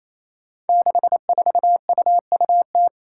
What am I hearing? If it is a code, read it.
Morse code, heard as a keyed tone, about 27 wpm, 700 Hz. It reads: 64UUT
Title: Five Challenging Letter-Number Combinations 28wpm